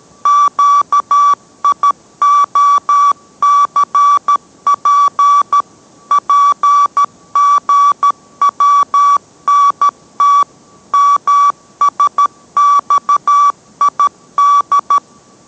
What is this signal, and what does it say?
Morse code, heard as a keyed tone, about 13 wpm, 1150 Hz. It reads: QIOCP PGWNT MSXID